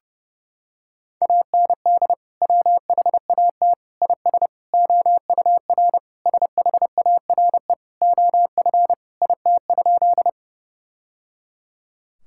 Morse code, heard as a keyed tone, about 32 wpm, 700 Hz. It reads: AND WHAT IS OUR SHARE OF IT?
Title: Sentences from Top 1000 Words - Review Part 6 - 30wpm